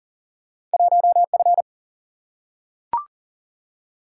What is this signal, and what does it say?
Morse code, heard as a keyed tone, about 40 wpm, 700 Hz. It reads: 1F
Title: Single-Letter Single-Number 40wpm